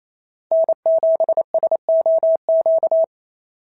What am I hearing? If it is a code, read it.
Morse code, heard as a keyed tone, about 28 wpm, 690 Hz. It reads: N7SOQ
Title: SOTA QSO Practice 28wpm